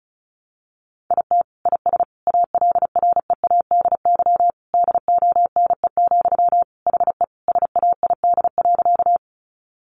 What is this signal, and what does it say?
Morse code, heard as a keyed tone, about 34 wpm, 700 Hz. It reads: IT IS ALREADY DONE, HE SAID.
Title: Sentences from Top 700 Words 35wpm